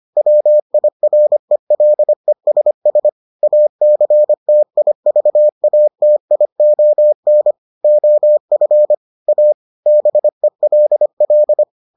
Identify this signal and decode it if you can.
Morse code, heard as a keyed tone, about 25 wpm, 600 Hz. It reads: WIRELESS ACTIVATION OF A BELL